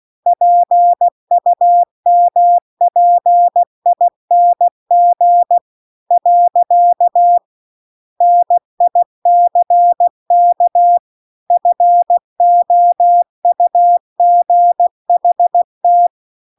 Morse code, heard as a keyed tone, about 16 wpm, 700 Hz. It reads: PUMPING . NICK FOUGHT